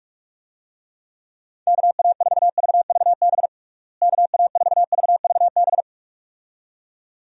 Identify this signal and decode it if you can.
Morse code, heard as a keyed tone, about 41 wpm, 700 Hz. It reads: KA4VVB KA4VVB
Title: USA Call Signs - 6 Characters - Repeated 2x - 45wpm